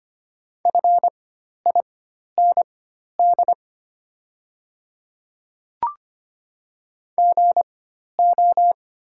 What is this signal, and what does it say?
Morse code, heard as a keyed tone, about 25 wpm, spaced as for 12 wpm, 700 Hz. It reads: FIND GO
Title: 4x Character Spacing - Top 100 Words - 25wpm